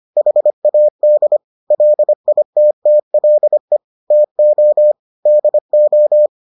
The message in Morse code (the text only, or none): HAD LITTLE TO DO